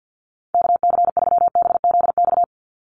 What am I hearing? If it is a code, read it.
Morse code, heard as a keyed tone, about 50 wpm, 700 Hz. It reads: KC3BZX
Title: USA Call Signs 50wpm